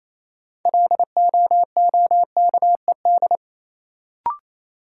Morse code, heard as a keyed tone, about 28 wpm, 700 Hz. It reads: LOOKED
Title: Sentences from Top 800 Words - New Word Drill - 28wpm